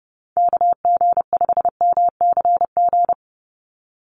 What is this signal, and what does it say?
Morse code, heard as a keyed tone, about 30 wpm, 700 Hz. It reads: KG5MCG
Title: USA Call Signs 30wpm